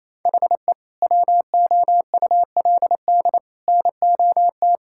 Morse code, heard as a keyed tone, about 28 wpm, 700 Hz. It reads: HE WOULD NOT
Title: Sentences from Top 1000 Words - Review Part 5 - 28wpm